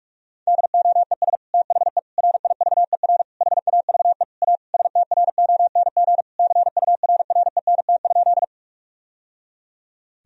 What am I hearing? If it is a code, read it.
Morse code, heard as a keyed tone, about 46 wpm, 700 Hz. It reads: DOES THE RIVER HAVE A STRONG CURRENT?
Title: Sentences from Top 1000 Words - Review Part 4 - 45wpm